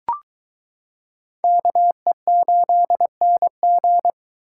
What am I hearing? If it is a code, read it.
Morse code, heard as a keyed tone, about 22 wpm, 700 Hz. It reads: KE8NG